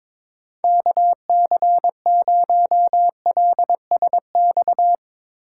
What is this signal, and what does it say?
Morse code, heard as a keyed tone, about 22 wpm, 700 Hz. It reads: KC0LSX
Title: USA Call Signs - 3x Character Spacing - 22wpm